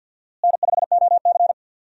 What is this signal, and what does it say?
Morse code, heard as a keyed tone, about 47 wpm, 700 Hz. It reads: N5OC
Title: USA Call Signs - 4 Characters - Repeated 2x - 50wpm